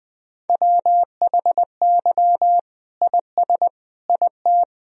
Morse code, heard as a keyed tone, about 21 wpm, 700 Hz. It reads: WHY IS IT
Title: Sentences from Top 1000 Words - Review Part 1 - 20wpm